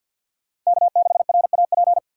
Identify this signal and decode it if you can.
Morse code, heard as a keyed tone, about 53 wpm, 700 Hz. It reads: K6RAP
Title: USA Call Signs - 5 Characters - 6x Character Spacing - 50wpm